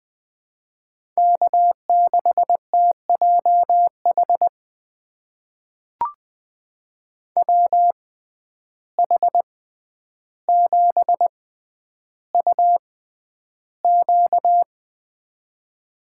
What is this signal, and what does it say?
Morse code, heard as a keyed tone, about 20 wpm, spaced as for 10 wpm, 700 Hz. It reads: K6TJH WH7UQ
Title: USA Call Signs - 6x Character Spacing - 20wpm